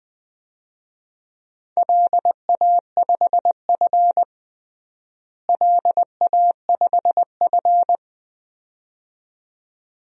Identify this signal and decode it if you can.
Morse code, heard as a keyed tone, about 20 wpm, 700 Hz. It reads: LA5F LA5F
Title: DXCC-100 Call Signs - 4 Characters - Repeated 2x - 20wpm